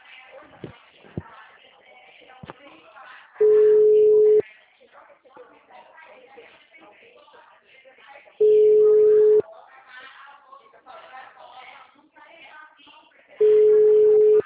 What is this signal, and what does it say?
Ringback tone (Europe, 425 Hz), with speech.